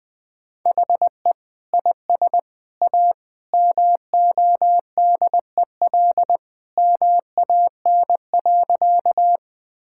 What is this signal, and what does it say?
Morse code, heard as a keyed tone, about 21 wpm, 700 Hz. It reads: HE IS A MODEL MAN.
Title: Sentences from Top 1000 Words - Review Part 5 - 20wpm